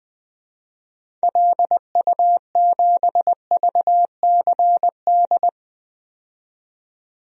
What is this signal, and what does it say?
Morse code, heard as a keyed tone, about 20 wpm, 700 Hz. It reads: LU7VCD
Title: DXCC-100 Call Signs - 6 Characters - 20wpm